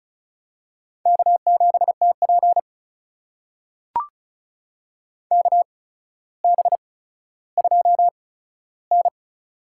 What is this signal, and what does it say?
Morse code, heard as a keyed tone, about 36 wpm, spaced as for 15 wpm, 700 Hz. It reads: K7TP KB2N
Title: USA Call Signs - 4 Characters - 8x Character Spacing - 35wpm